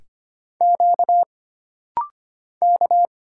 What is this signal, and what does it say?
Morse code, heard as a keyed tone, about 25 wpm, 700 Hz. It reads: Q K
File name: Single Character (Letter:Number:Punctuation:Prosign) - Rapid-Fire - 25wpm